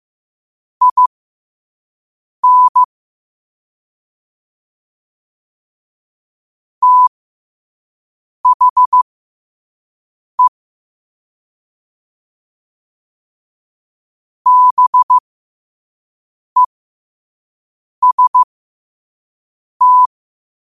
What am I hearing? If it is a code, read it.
Morse code, heard as a keyed tone, about 15 wpm, spaced as for 5 wpm, 1000 Hz. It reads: IN THE BEST
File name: qotdfiles / 5wpm